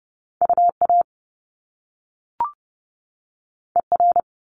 Morse code, heard as a keyed tone, about 30 wpm, 700 Hz. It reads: UA ER
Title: Top Two Letter Combinations in Words 30wpm